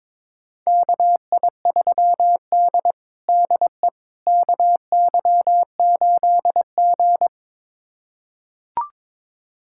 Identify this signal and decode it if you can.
Morse code, heard as a keyed tone, about 22 wpm, 700 Hz. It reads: KI3D DE KY8G
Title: USA Callsign DE Callsign - 4 Characters - 22wpm